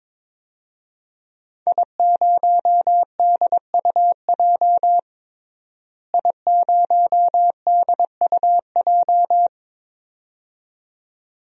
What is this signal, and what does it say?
Morse code, heard as a keyed tone, about 22 wpm, 700 Hz. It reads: I0DUJ I0DUJ
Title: DXCC-100 Call Signs - 5 Characters - Repeated 2x - 22wpm